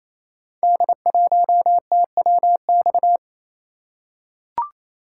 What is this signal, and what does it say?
Morse code, heard as a keyed tone, about 29 wpm, 700 Hz. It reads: D1TWX
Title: Code Groups - 5 Letter:Numbers - 28wpm